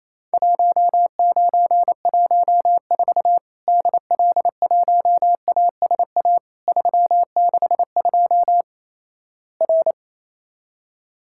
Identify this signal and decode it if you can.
Morse code, heard as a keyed tone, about 28 wpm, 700 Hz. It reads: 1914 DL1ASA 362 R
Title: Worked All Europe DX Contest - QTC Exchange - 28wpm